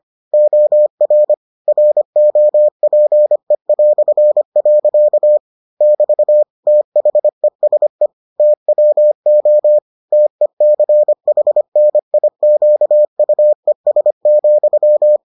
Morse code, heard as a keyed tone, about 25 wpm, 600 Hz. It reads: OR ROPE". <BT> THESE TWO TECHNIQUES,